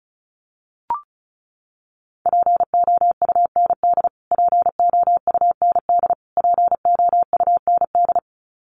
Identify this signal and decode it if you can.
Morse code, heard as a keyed tone, about 34 wpm, 700 Hz. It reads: POUND POUND POUND
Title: Top 500 Words Repeated 35wpm